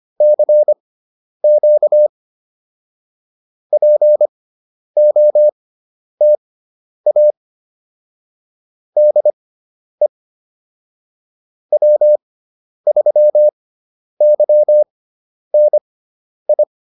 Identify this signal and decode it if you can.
Morse code, heard as a keyed tone, about 25 wpm, spaced as for 10 wpm, 600 Hz. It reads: CQ POTA DE W3YNI